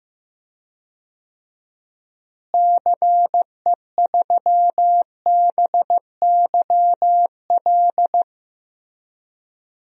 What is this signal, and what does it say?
Morse code, heard as a keyed tone, about 15 wpm, 700 Hz. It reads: CE3BYL
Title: DXCC-100 Call Signs - 6 Characters - Repeated 3x - 15wpm